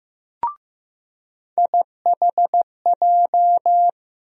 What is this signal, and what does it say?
Morse code, heard as a keyed tone, about 15 wpm, 700 Hz. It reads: IHJ
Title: USA Call Sign Postfixes - 3 Letters 15wpm